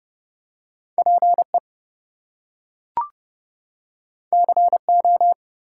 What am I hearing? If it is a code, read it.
Morse code, heard as a keyed tone, about 29 wpm, 700 Hz. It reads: PE CO
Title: US States and Canadian Provinces 30wpm